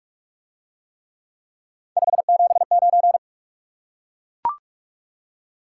Morse code, heard as a keyed tone, about 44 wpm, 700 Hz. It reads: 579